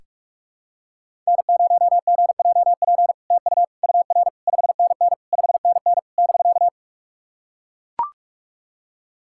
Morse code, heard as a keyed tone, about 44 wpm, 700 Hz. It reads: N0GJP TU UR 5NN 5NN <BK>
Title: SOTA QSO Practice 45wpm